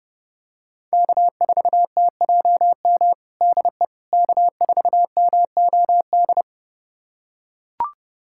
Morse code, heard as a keyed tone, about 30 wpm, 700 Hz. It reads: K4TJM DE K4MOD
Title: USA Callsign DE Callsign - 5 Characters - 30wpm